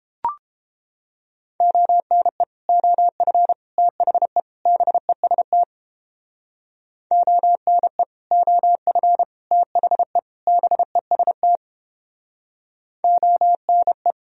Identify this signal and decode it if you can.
Morse code, heard as a keyed tone, about 31 wpm, 700 Hz. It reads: ONE OF THE BEST ONE OF THE BEST ONE